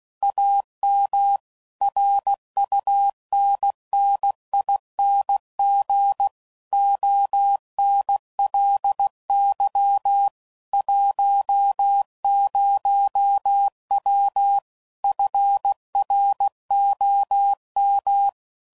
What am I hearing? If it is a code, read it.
Morse code, heard as a keyed tone, about 16 wpm, 800 Hz. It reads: AM RUNNING ONLY 10W FROM
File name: CW Code Groups @15WPM / QSOs / kc7haa